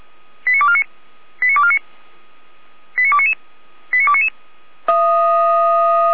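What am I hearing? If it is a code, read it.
Five-tone selective call (ZVEI), heard as quick runs of five tones, tones of about 80 ms. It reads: 89249 89249 89299 89299
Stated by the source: Five-tone ZVEI selective call received on SDR, code 89250.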